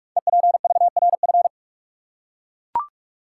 Morse code, heard as a keyed tone, about 44 wpm, 690 Hz. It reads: EPVRF
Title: Code Groups - 5 Letters - 45wpm